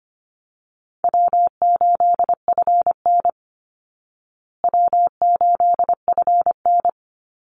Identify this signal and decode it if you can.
Morse code, heard as a keyed tone, about 25 wpm, 700 Hz. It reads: W8FN W8FN
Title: Easy Call Signs Repeated 3x - 25wpm